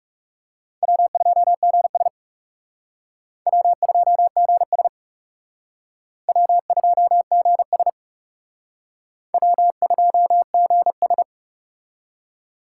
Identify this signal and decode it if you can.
Morse code, heard as a keyed tone, about 35 wpm, 700 Hz. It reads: W2GS W2GS W2GS W2GS
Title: Speed Racing - USA Call Signs 30wpm